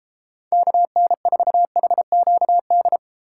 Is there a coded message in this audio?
Yes: Morse code from 0.5 to 3.0 s, heard as a keyed tone, about 33 wpm, 700 Hz. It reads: KN4HQD